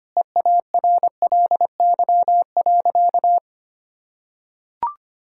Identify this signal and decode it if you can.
Morse code, heard as a keyed tone, about 25 wpm, 700 Hz. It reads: EARLY.
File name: practice / 25wpm / Sentences with 3-Word Phrases - 25wpm